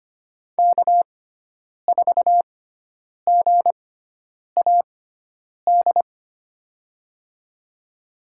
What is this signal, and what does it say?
Morse code, heard as a keyed tone, about 25 wpm, spaced as for 9 wpm, 700 Hz. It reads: K4GAD
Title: USA Call Signs - 6x Character Spacing - 25wpm